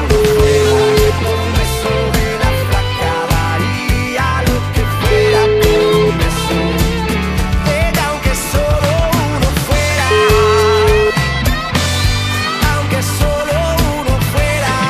Ringback tone (Europe, 425 Hz), with noise.